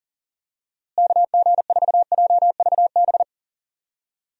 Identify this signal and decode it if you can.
Morse code, heard as a keyed tone, about 40 wpm, 700 Hz. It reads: KG4JVB
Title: USA Call Signs - 6 Characters - Repeated 2x - 40wpm